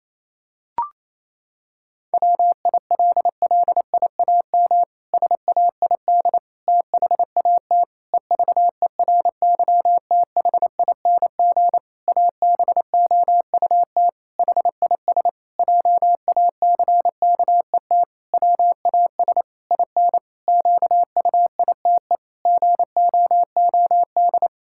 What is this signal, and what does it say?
Morse code, heard as a keyed tone, about 27 wpm, 700 Hz. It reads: WILLIAM SAID THAT EVERYTHING ABOUT HIS JACKET WAS IN QUITE GOOD
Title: Pangram Challenge v2 - 28wpm